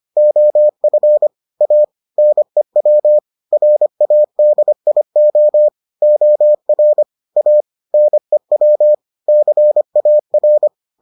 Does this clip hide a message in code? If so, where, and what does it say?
Yes: Morse code from 0.2 to 10.7 s, heard as a keyed tone, about 25 wpm, 600 Hz. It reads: OF A NEW RADIO OR A NEW CAR